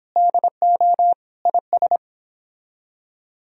Morse code, heard as a keyed tone, about 31 wpm, 700 Hz. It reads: DO IS